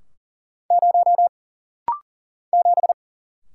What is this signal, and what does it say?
Morse code, heard as a keyed tone, about 40 wpm, 700 Hz. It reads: 0 7
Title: Single Letter-Number - Rapid-Fire 40wpm